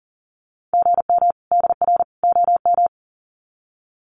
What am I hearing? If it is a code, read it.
Morse code, heard as a keyed tone, about 40 wpm, 700 Hz. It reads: GM DR OM
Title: Common QSO Phrases 40wpm